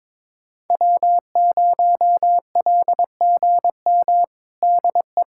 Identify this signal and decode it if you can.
Morse code, heard as a keyed tone, about 22 wpm, 700 Hz. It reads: W0LGM DE
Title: USA Callsign DE Callsign - 5 Characters - 22wpm